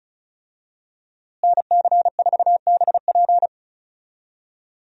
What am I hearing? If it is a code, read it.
Morse code, heard as a keyed tone, about 34 wpm, 700 Hz. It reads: NC4BP
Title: USA Call Signs - 5 Characters - Repeated 2x - 35wpm